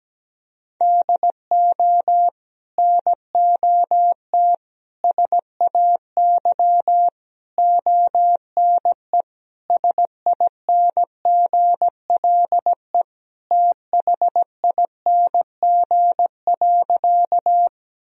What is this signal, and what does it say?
Morse code, heard as a keyed tone, about 17 wpm, 700 Hz. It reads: DO NOT SAY ONE SINGLE THING.
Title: Sentences from Top 1000 Words - Review Part 4 - 17wpm